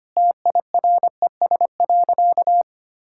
Morse code, heard as a keyed tone, about 25 wpm, 700 Hz. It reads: TIRES.
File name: Easy Sentences with Hints - 25wpm